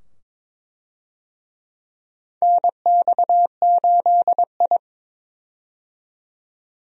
Morse code, heard as a keyed tone, about 22 wpm, 700 Hz. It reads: NX8I
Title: USA Call Signs - 4 Characters - 22wpm